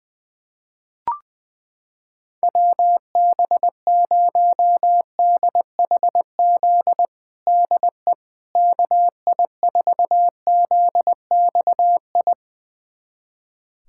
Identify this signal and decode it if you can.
Morse code, heard as a keyed tone, about 21 wpm, 700 Hz. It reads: WB0DHZ DE KI4ZXI
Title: USA Callsign DE Callsign - 6 Characters - 20wpm